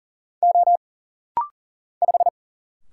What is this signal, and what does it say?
Morse code, heard as a keyed tone, about 40 wpm, 700 Hz. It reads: O 5
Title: Single Letter-Number Alt - Rapid-Fire 40wpm